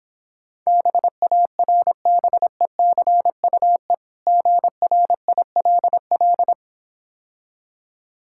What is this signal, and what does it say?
Morse code, heard as a keyed tone, about 26 wpm, 700 Hz. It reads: BARBECUE GRILL